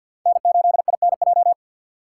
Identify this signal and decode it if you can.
Morse code, heard as a keyed tone, about 61 wpm, 700 Hz. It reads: N8INJ
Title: Speed Racing - USA Call Signs 50wpm